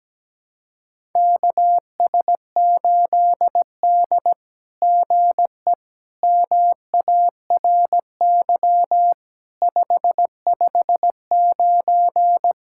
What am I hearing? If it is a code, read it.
Morse code, heard as a keyed tone, about 17 wpm, 700 Hz. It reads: KS8D GE MARY 559
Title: SOTA QSO Practice 17wpm